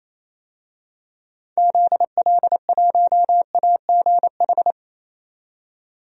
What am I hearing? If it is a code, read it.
Morse code, heard as a keyed tone, about 27 wpm, 700 Hz. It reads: ZL1AGH
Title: DXCC-100 Call Signs - 6 Characters - 28wpm